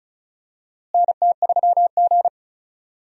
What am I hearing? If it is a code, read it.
Morse code, heard as a keyed tone, about 36 wpm, 700 Hz. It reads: NT3G
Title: USA Call Signs - 4 Characters - Repeated 3x - 35wpm